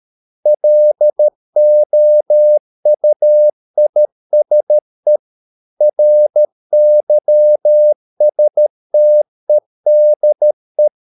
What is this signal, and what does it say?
Morse code, heard as a keyed tone, about 13 wpm, 600 Hz. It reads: LOUISE RYSTEDE
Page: Ferie 13 wpm